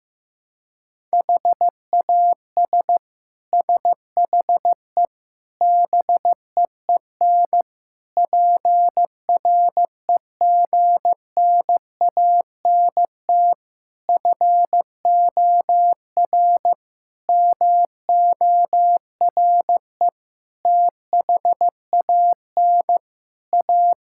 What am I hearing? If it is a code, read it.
Morse code, heard as a keyed tone, about 16 wpm, 700 Hz. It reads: HAS SHE BEEN PREGNANT FOR MORE THAN A
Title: Sentences from Top 1000 Words - Review Part 6 - 15wpm